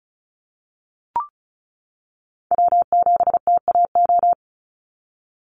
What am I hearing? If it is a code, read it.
Morse code, heard as a keyed tone, about 34 wpm, 700 Hz. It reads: W7TAO
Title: NA SOTA Chaser Callsigns 35wpm